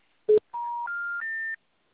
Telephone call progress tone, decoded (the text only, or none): special information tone